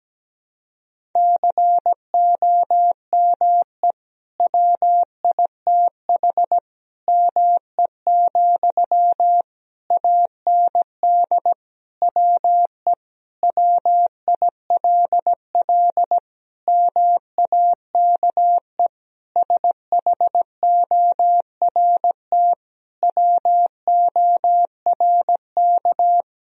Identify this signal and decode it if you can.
Morse code, heard as a keyed tone, about 17 wpm, 700 Hz. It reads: COME WITH ME, AND WE WILL MAKE SHORT WORK
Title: Sentences from Top 1000 Words - Review Part 2 - 17wpm